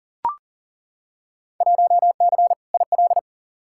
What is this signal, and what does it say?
Morse code, heard as a keyed tone, about 40 wpm, 700 Hz. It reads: 1C IL
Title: Field Day Exchanges - 40wpm